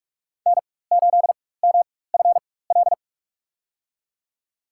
Morse code, heard as a keyed tone, about 41 wpm, spaced as for 20 wpm, 700 Hz. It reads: N8MFL
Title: USA Call Signs - 5 Characters - 4x Character Spacing - 45wpm